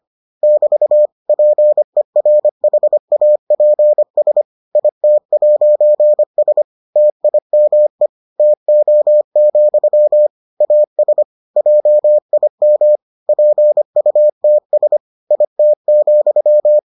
Morse code, heard as a keyed tone, about 25 wpm, 600 Hz. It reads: <BT> PERHAPS IT'S TIME TO, AS JIM PUTS IT,